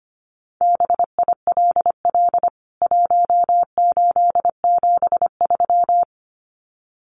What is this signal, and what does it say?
Morse code, heard as a keyed tone, about 25 wpm, 700 Hz. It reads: BILL 1873
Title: CWT Exchange 25wpm